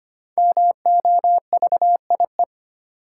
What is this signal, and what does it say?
Morse code, heard as a keyed tone, about 25 wpm, 700 Hz. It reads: MOVIE
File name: 5 Letter Words - First 3 Letters Given - 25wpm